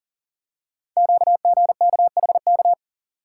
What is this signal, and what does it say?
Morse code, heard as a keyed tone, about 40 wpm, 700 Hz. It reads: QGKHK
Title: Code Groups - 5 Letters - 40wpm